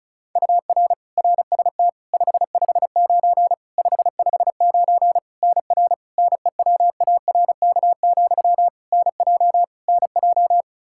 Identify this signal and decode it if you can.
Morse code, heard as a keyed tone, about 36 wpm, 700 Hz. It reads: UR RST 559 559 NR NEWARK, NJ NJ
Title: K1USN SST QSO Practice 35wpm